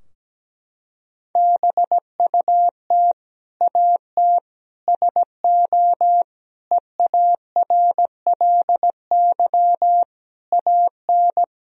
Morse code, heard as a keyed tone, about 17 wpm, 700 Hz. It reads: BUT AT SO EARLY AN
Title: Sentences from Top 1000 Words - Review Part 3 - 17wpm